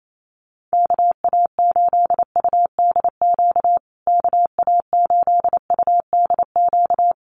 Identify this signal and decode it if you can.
Morse code, heard as a keyed tone, about 28 wpm, 700 Hz. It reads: KA8UDQ KA8UDQ
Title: USA Call Signs Repeated 3x 28wpm